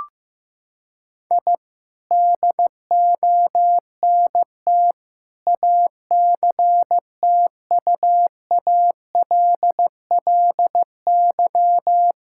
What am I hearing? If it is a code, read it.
Morse code, heard as a keyed tone, about 15 wpm, 700 Hz. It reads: I DONT ACTUALLY
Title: Sentences from Top 1000 Words - Review Part 5 - 15wpm